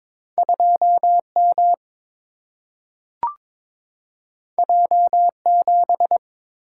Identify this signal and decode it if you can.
Morse code, heard as a keyed tone, about 22 wpm, 700 Hz. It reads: 2M J7
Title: Single-Letter Single-Number 22wpm